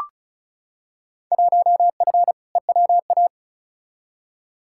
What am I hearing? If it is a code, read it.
Morse code, heard as a keyed tone, about 36 wpm, 700 Hz. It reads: 1F EWA
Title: Field Day Exchanges Encore - 35wpm